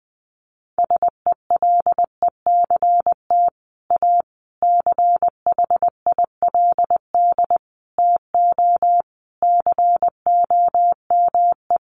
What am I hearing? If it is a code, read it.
Morse code, heard as a keyed tone, about 20 wpm, 700 Hz. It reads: SELECT A CHILD TO COME
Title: Sentences from Top 1300 Words 20wpm